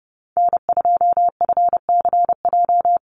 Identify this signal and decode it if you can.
Morse code, heard as a keyed tone, about 30 wpm, 700 Hz. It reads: N2FCJ
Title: USA Call Signs 30wpm